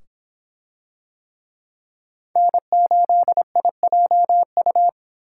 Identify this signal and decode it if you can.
Morse code, heard as a keyed tone, about 25 wpm, 700 Hz. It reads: N8IJU